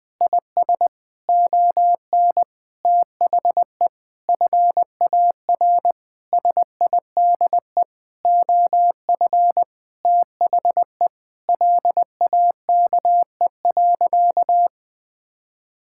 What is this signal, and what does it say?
Morse code, heard as a keyed tone, about 20 wpm, 700 Hz. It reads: IS ON THE FAR SIDE OF THE LAKE.
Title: Sentences with 5-Word Phrases - Spoken Beginning - 20wpm